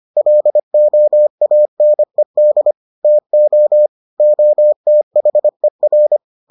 Morse code, heard as a keyed tone, about 25 wpm, 600 Hz. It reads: LOANED TO OTHER